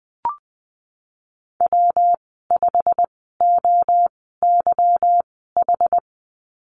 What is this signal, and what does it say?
Morse code, heard as a keyed tone, about 19 wpm, spaced as for 14 wpm, 700 Hz. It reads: W5OYH
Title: 193 Lesson 31g - Letter v - Call Signs 20wpm